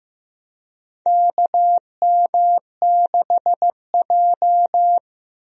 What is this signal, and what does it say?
Morse code, heard as a keyed tone, about 15 wpm, 700 Hz. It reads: KM6J
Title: USA Call Signs - 4 Characters - 15wpm